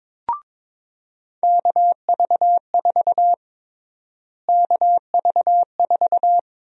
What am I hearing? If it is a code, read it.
Morse code, heard as a keyed tone, about 21 wpm, 700 Hz. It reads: KV4 KV4
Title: USA Call Sign Prefixes - Repeated 2x - 22wpm